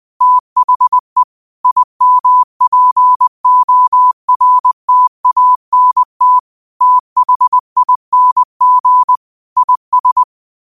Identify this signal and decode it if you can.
Morse code, heard as a keyed tone, about 20 wpm, 1000 Hz. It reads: THE IMPORTANT THING IS
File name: qotdfiles / 20wpm